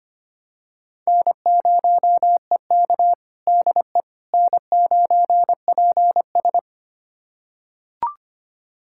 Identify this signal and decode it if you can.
Morse code, heard as a keyed tone, about 25 wpm, 700 Hz. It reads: N0EK DE N9PS
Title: USA Callsign DE Callsign - 4 Characters - 25wpm